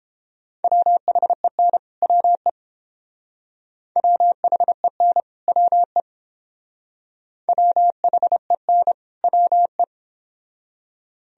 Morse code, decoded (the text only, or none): WHEN WE WHEN WE WHEN WE